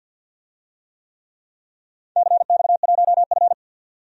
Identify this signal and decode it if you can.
Morse code, heard as a keyed tone, about 52 wpm, 700 Hz. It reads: CX1F
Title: DXCC-100 Call Signs - 4 Characters - 50wpm